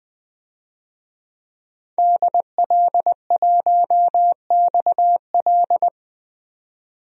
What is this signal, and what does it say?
Morse code, heard as a keyed tone, about 20 wpm, 700 Hz. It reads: DL1XL
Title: DXCC-100 Call Signs - 5 Characters - 20wpm